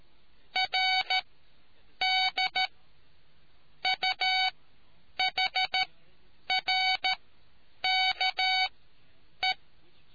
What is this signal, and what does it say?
Morse code, heard as a keyed tone, about 13 wpm, spaced as for 8 wpm, 750 Hz. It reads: RD UHRKE